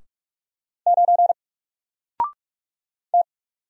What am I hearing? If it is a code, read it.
Morse code, heard as a keyed tone, about 44 wpm, 700 Hz. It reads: 9 T
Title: Single Character (Letter:Number:Punctuation:Prosign) Alt 45wpm